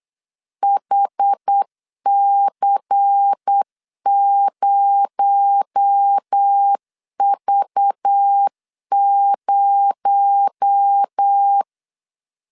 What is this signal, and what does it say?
Morse code, heard as a keyed tone, about 8 wpm, 800 Hz. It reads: HC0V0
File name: LEZIONI / 6 PKX0 / 6 PKX0 8wpm 3341